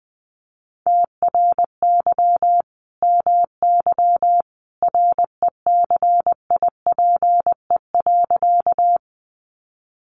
Morse code, heard as a keyed tone, about 20 wpm, 700 Hz. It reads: TRY MY RECIPE.
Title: Sentences from Top 1200 Words 20wpm